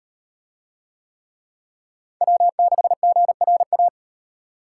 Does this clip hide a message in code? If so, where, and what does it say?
Yes: Morse code from 2.2 to 3.9 s, heard as a keyed tone, about 38 wpm, 700 Hz. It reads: W6GRA